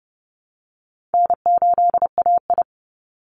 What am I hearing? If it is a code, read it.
Morse code, heard as a keyed tone, about 30 wpm, 700 Hz. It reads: N8AI
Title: Easy Call Signs Repeated 2x - 30wpm